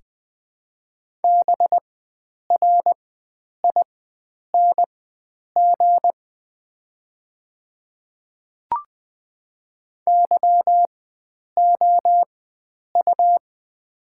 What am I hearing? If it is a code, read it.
Morse code, heard as a keyed tone, about 20 wpm, spaced as for 9 wpm, 700 Hz. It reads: BRING YOU
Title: 4x Character Spacing - Top 500 Words - 20wpm